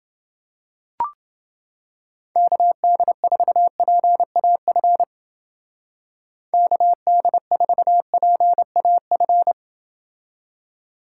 Morse code, heard as a keyed tone, about 30 wpm, 700 Hz. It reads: KD4PAF KD4PAF